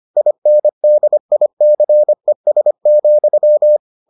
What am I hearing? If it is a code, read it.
Morse code, heard as a keyed tone, about 25 wpm, 600 Hz. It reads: INDICES,